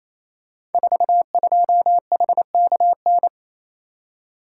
Morse code, heard as a keyed tone, about 29 wpm, 700 Hz. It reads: 42HKN
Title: Code Groups - 5 Letter:Numbers - 28wpm